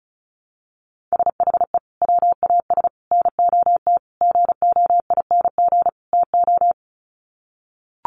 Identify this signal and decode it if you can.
Morse code, heard as a keyed tone, about 34 wpm, 700 Hz. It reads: SHE WAS NOT GOING TO
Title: Sets of 5 Words Encore2 35wpm